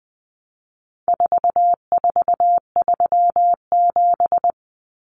Morse code, heard as a keyed tone, about 20 wpm, 700 Hz. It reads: 4437
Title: Numbers - 4-Digit 20wpm